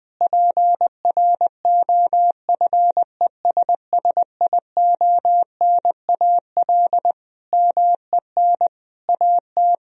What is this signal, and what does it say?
Morse code, heard as a keyed tone, about 20 wpm, 700 Hz. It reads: PROFESSIONAL MEN AT
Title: Sentences from Top 1000 Words - Review Part 4 - 20wpm